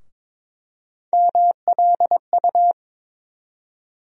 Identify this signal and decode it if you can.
Morse code, heard as a keyed tone, about 22 wpm, 700 Hz. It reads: MLU
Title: USA Call Sign Postfixes - 3 Letters 22wpm